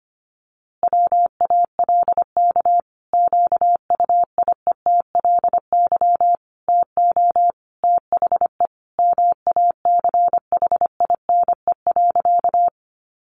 Morse code, heard as a keyed tone, about 25 wpm, 700 Hz. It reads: WALK QUIETLY TO THE MACHINE.
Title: A Sound Of Thunder 25wpm